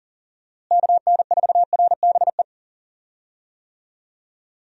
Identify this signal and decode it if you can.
Morse code, heard as a keyed tone, about 40 wpm, 700 Hz. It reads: KN4RBE
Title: USA Call Signs - 6 Characters - Repeated 3x - 40wpm